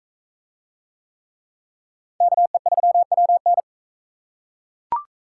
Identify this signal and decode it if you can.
Morse code, heard as a keyed tone, about 41 wpm, 700 Hz. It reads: KE3WN